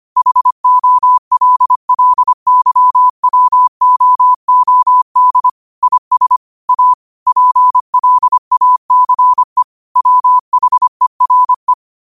Morse code, heard as a keyed tone, about 25 wpm, 1000 Hz. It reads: HOLLYWOOD IS A PLACE WHERE
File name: qotdfiles / 25wpm